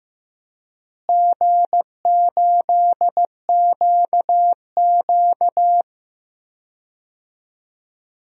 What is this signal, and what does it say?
Morse code, heard as a keyed tone, about 15 wpm, 700 Hz. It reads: G8QQ
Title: DXCC-100 Call Signs - 4 Characters - Repeated 2x - 15wpm